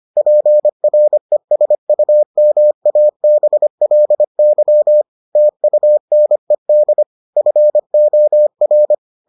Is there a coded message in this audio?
Yes: Morse code from 0.2 to 8.9 s, heard as a keyed tone, about 25 wpm, 600 Hz. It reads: PRESUMABLY TUNED FOR